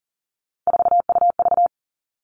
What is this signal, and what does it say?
Morse code, heard as a keyed tone, about 40 wpm, 690 Hz. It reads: 4UV
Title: Three Challenging Letter-Number Combinations 40wpm